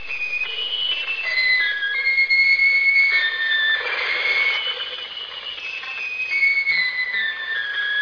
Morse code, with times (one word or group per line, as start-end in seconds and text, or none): none